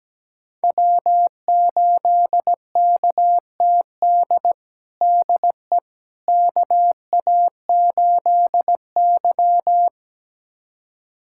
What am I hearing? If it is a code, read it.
Morse code, heard as a keyed tone, about 17 wpm, 700 Hz. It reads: W8KTD DE KA8Y
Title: USA Callsign DE Callsign 17wpm